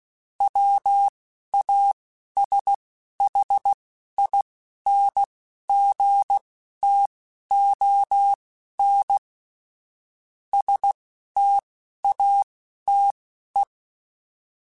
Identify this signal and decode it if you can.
Morse code, heard as a keyed tone, about 16 wpm, spaced as for 12 wpm, 800 Hz. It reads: WASHINGTON STATE
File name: QSOs / QSOs-mp3s / WD0BGP 1@15